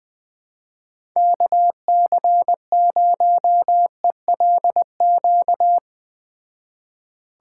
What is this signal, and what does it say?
Morse code, heard as a keyed tone, about 20 wpm, 700 Hz. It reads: KC0ELQ
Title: Speed Racing - USA 2x3 Call Signs 17wpm